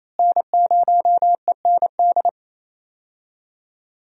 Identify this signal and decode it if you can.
Morse code, heard as a keyed tone, about 28 wpm, 700 Hz. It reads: N0END
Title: USA Call Signs - 5 Characters - Repeated 3x - 28wpm